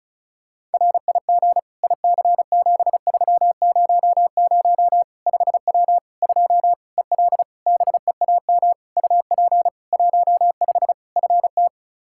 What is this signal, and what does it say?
Morse code, heard as a keyed tone, about 36 wpm, 700 Hz. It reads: RIG IC7300 5W 2 EL BEAM UP 15 FT
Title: K1USN SST QSO Practice 35wpm